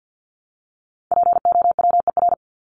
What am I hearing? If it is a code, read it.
Morse code, heard as a keyed tone, about 53 wpm, 700 Hz. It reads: POWER
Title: Top 500 Words 50wpm